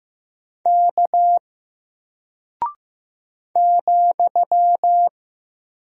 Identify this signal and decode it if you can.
Morse code, heard as a keyed tone, about 15 wpm, 700 Hz. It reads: K ,
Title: Single Character (Letter:Number:Punctuation:Prosign) Alt 15wpm